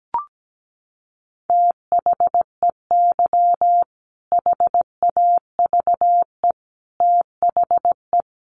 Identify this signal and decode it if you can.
Morse code, heard as a keyed tone, about 17 wpm, 700 Hz. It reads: THEY HAVE THE
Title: Sentences from Top 1300 Words 17wpm